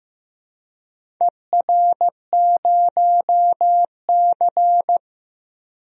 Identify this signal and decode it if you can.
Morse code, heard as a keyed tone, about 15 wpm, 700 Hz. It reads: ER0C
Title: DXCC-100 Call Signs - 4 Characters - Repeated 3x - 15wpm